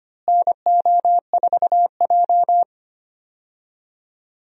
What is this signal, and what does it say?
Morse code, heard as a keyed tone, about 25 wpm, 700 Hz. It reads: NO4J
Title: USA Call Signs - 4 Characters - Repeated 3x - 25wpm